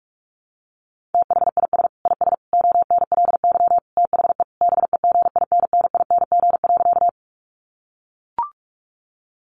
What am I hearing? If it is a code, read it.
Morse code, heard as a keyed tone, about 44 wpm, 700 Hz. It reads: THIS IS ONLY THE BEGINNING.
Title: Sentences from Top 800 Words 45wpm